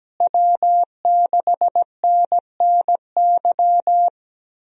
Morse code, heard as a keyed tone, about 17 wpm, 700 Hz. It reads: W6NNY
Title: USA Call Signs - 5 Characters - Repeated 3x - 17wpm